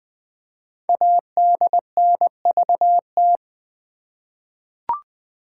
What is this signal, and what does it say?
Morse code, heard as a keyed tone, about 20 wpm, 700 Hz. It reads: ADNVT
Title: Code Groups - 5 Letters - 20wpm